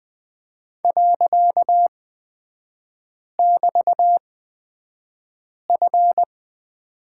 Morse code, heard as a keyed tone, about 20 wpm, spaced as for 5 wpm, 700 Hz. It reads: .<BT>F
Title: 255 Lesson 41e - Period - Three Characters 20wpm